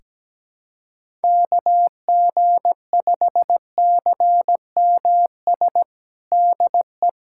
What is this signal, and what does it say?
Morse code, heard as a keyed tone, about 17 wpm, 700 Hz. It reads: KG5CMS DE
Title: USA Callsign DE Callsign - 6 Characters - 17wpm